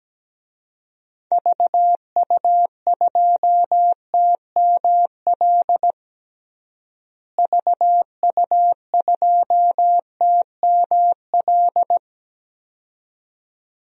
Morse code, heard as a keyed tone, about 17 wpm, 700 Hz. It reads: VU2TML VU2TML
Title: DXCC-100 Call Signs - 6 Characters - Repeated 2x - 17wpm